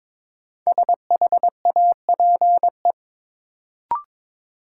Morse code, heard as a keyed tone, about 22 wpm, 700 Hz. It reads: SHAPE
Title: Sentences from Top 500 Words - New Word Drill - 22wpm